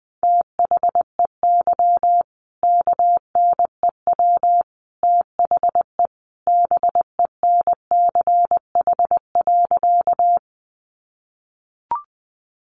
Morse code, heard as a keyed tone, about 20 wpm, 700 Hz. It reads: THEY KNEW THE BENCH.
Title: Sentences from Top 1400 Words 20wpm